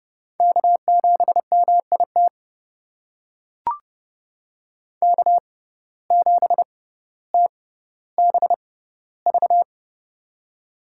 Morse code, heard as a keyed tone, about 31 wpm, spaced as for 15 wpm, 700 Hz. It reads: K7MIT K7TBV
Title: USA Call Signs - 5 Characters - 6x Character Spacing - 30wpm